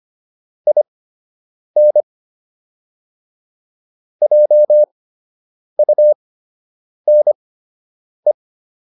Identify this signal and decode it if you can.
Morse code, heard as a keyed tone, about 25 wpm, spaced as for 8 wpm, 600 Hz. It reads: IN JUNE